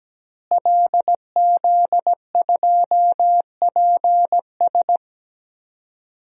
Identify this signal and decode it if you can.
Morse code, heard as a keyed tone, about 17 wpm, 700 Hz. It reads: LZ2PS
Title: DXCC-100 Call Signs - 5 Characters - Repeated 2x - 17wpm